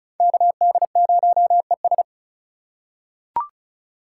Morse code, heard as a keyed tone, about 36 wpm, 700 Hz. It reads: KD0ES
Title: USA Call Signs - 5 Characters - 2x Character Spacing - 35wpm